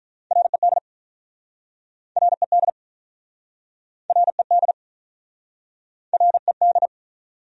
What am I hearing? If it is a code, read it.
Morse code, heard as a keyed tone, about 47 wpm, 700 Hz. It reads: RED RED RED RED